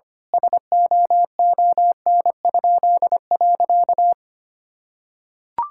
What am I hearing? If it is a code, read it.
Morse code, heard as a keyed tone, about 25 wpm, 700 Hz. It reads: SOON?.
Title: Pangram Challenge v2 - 25wpm